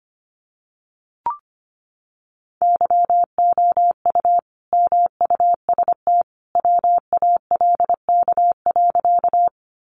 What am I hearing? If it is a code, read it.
Morse code, heard as a keyed tone, about 25 wpm, 700 Hz. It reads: YOU MUST WALK.
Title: Sentences from Top 300 Words 25wpm